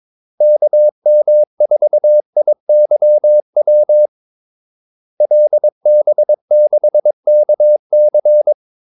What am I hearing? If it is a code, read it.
Morse code, heard as a keyed tone, about 22 wpm, 600 Hz. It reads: KM4IYW LB6KC
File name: Callsigns-22wpm